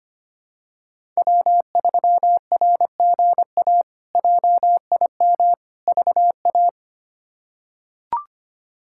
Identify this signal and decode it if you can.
Morse code, heard as a keyed tone, about 25 wpm, 700 Hz. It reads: W3RGA JIM VA
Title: K1USN SST QSO Practice - 25wpm